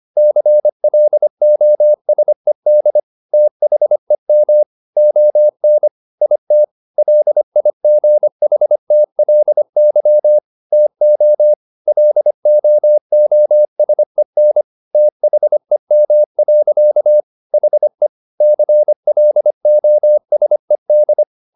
Morse code, heard as a keyed tone, about 25 wpm, 600 Hz. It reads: CLOSED THEM ON IT LIGHTLY TO LOOSEN THEM. HE CLOSED